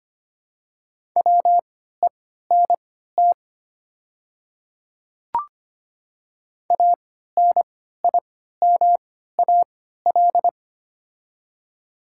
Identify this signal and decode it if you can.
Morse code, heard as a keyed tone, about 25 wpm, spaced as for 14 wpm, 690 Hz. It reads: WENT ANIMAL
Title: 3x Character Spacing - Top 200 Words - 25wpm